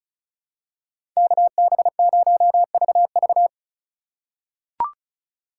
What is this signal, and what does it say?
Morse code, heard as a keyed tone, about 36 wpm, 700 Hz. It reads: KB0VV
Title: USA Call Signs - 5 Characters - 2x Character Spacing - 35wpm